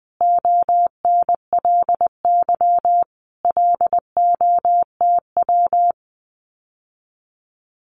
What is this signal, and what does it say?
Morse code, heard as a keyed tone, about 20 wpm, 700 Hz. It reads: ONLY LOTW
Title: Common QSO Phrases Encore 20wpm